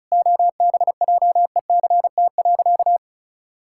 Morse code, heard as a keyed tone, about 36 wpm, 700 Hz. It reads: OBJECT.
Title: Sentences from Top 1000 Words - Review Part 5 - 35wpm